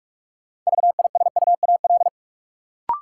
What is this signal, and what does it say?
Morse code, heard as a keyed tone, about 46 wpm, 700 Hz. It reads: VISUAL
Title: Sets of 5 Words Encore - New Word Drill - 45wpm